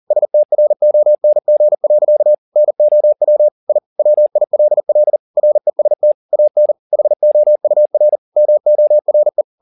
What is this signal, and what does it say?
Morse code, heard as a keyed tone, about 40 wpm, 600 Hz. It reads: STRONG. NOW I WILL REST AN HOUR MORE
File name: the old man and the sea 2 40wpm